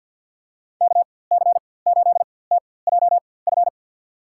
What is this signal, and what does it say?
Morse code, heard as a keyed tone, about 52 wpm, spaced as for 23 wpm, 700 Hz. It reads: KC8TJF
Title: USA Call Signs - 6 Characters - 4x Character Spacing - 50wpm